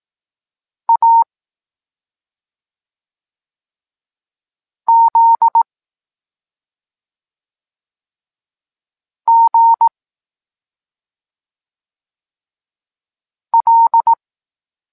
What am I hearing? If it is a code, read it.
Morse code, heard as a keyed tone, about 18 wpm, 920 Hz. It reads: A Z G L
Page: Tekst 23.3 – Ciąg dalszy doskonalenia odbioru poznanych liter